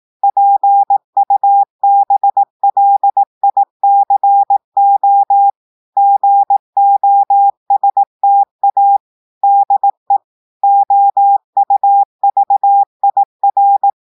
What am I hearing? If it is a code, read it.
Morse code, heard as a keyed tone, about 18 wpm, 800 Hz. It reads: PUBLICO GOSTA DE OUVIR
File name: portugais texte en clair 2 vitesse 18 mots minutes